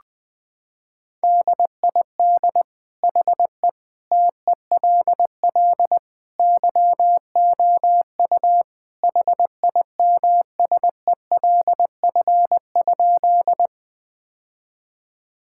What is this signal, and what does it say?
Morse code, heard as a keyed tone, about 20 wpm, 700 Hz. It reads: DID HE TELL YOU HIMSELF?
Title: Sentences from Top 1600 Words 20wpm